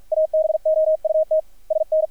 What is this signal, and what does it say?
Morse code, heard as a keyed tone, about 50 wpm, 630 Hz, with radio noise. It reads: ABOUT SN
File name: archived / CRSnet